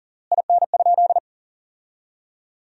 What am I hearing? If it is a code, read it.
Morse code, heard as a keyed tone, about 40 wpm, 700 Hz. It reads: IN?
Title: 3x Word-Spacing - Sentences from Top 300 Words - 40wpm